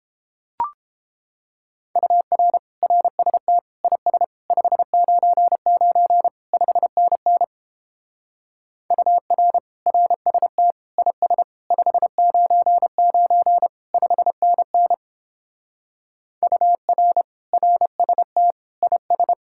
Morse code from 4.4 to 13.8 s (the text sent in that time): 599 5NN UR RST IS 599